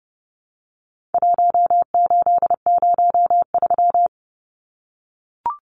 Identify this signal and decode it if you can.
Morse code, heard as a keyed tone, about 30 wpm, 700 Hz. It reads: 1803
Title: Numbers - 4-Digit 30wpm